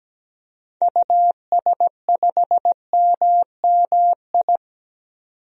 Morse code, heard as a keyed tone, about 17 wpm, 700 Hz. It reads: US5MMI
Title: DXCC-100 Call Signs - 6 Characters - 17wpm